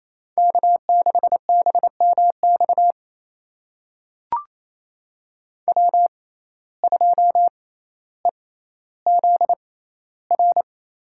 Morse code, heard as a keyed tone, about 28 wpm, spaced as for 14 wpm, 700 Hz. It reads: K6BMX W2EZR
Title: USA Call Signs - 5 Characters - 6x Character Spacing - 28wpm